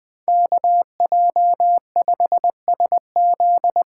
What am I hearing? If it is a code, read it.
Morse code, heard as a keyed tone, about 25 wpm, 700 Hz. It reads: KJ5SZ
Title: Speed Racing - USA Call Signs 20wpm